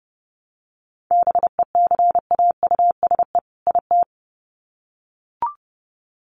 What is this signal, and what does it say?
Morse code, heard as a keyed tone, about 30 wpm, 700 Hz. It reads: BECAUSE IT
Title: Sets of 2 Words 30wpm